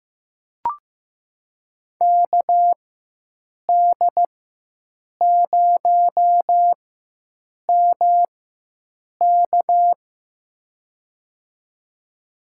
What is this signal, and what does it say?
Morse code, heard as a keyed tone, about 15 wpm, spaced as for 7 wpm, 700 Hz. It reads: KD0MK
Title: USA Call Signs - 5 Characters - 4x Character Spacing - 15wpm